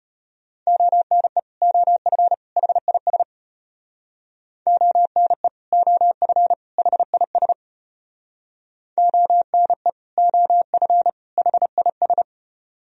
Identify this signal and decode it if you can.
Morse code, heard as a keyed tone, about 31 wpm, 700 Hz. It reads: ONE OF HIS ONE OF HIS ONE OF HIS
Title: Speed-Racing - Sets of 3 Words 25wpm